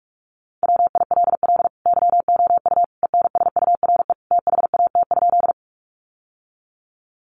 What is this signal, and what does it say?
Morse code, heard as a keyed tone, about 44 wpm, 700 Hz. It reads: WILL YOU ENSURE THAT?
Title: Sentences from Top 1100 Words 45wpm